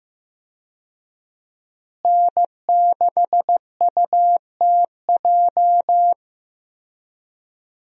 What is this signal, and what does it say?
Morse code, heard as a keyed tone, about 15 wpm, 700 Hz. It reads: N6UTJ
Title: USA Call Signs - 5 Characters - Repeated 2x - 15wpm